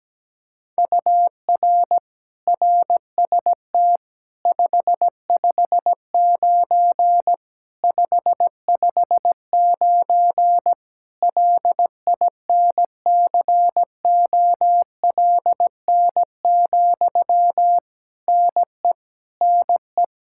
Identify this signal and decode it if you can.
Morse code, heard as a keyed tone, about 17 wpm, 700 Hz. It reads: UR RST 559 559 LINCOLN, NE NE
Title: Practice QSO - Exchange 1 - Beginning - 17wpm